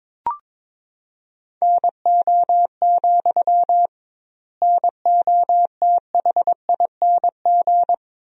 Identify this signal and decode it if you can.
Morse code, heard as a keyed tone, about 22 wpm, 700 Hz. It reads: NO, NOTHING
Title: 2x Word-Spacing - Sentences from Top 500 Words - 22wpm